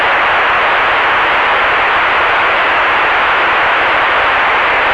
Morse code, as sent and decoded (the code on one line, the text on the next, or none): none